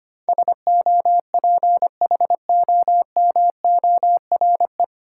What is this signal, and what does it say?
Morse code, heard as a keyed tone, about 25 wpm, 690 Hz. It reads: SOPHOMORE
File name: practice / 25wpm / 9 Letter Words - First 3 Letters Given - 25wpm